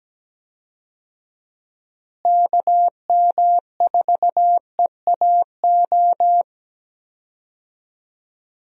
Morse code, heard as a keyed tone, about 17 wpm, 700 Hz. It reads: KM4EAO
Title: USA Call Signs - 6 Characters - Repeated 3x - 17wpm